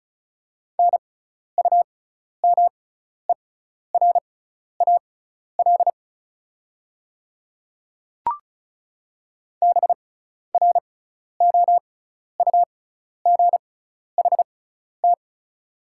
Morse code, heard as a keyed tone, about 36 wpm, spaced as for 12 wpm, 690 Hz. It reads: NUMERAL BROUGHT
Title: 6x Character Spacing - Top 500 Words - 35wpm